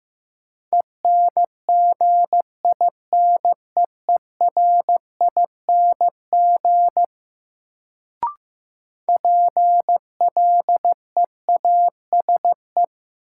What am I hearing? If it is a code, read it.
Morse code, heard as a keyed tone, about 15 wpm, 700 Hz. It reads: ENGINEERING PLEASE
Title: Sentences from Top 1500 Words - New Word Drill - 15wpm